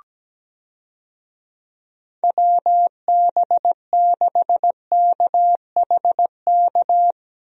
Morse code, heard as a keyed tone, about 17 wpm, 700 Hz. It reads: WB6KHK
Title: USA Call Signs - 6 Characters - Repeated 3x - 17wpm